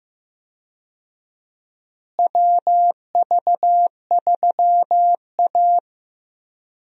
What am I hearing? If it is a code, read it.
Morse code, heard as a keyed tone, about 15 wpm, 700 Hz. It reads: WV3A
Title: USA Call Signs - 4 Characters - Repeated 3x - 15wpm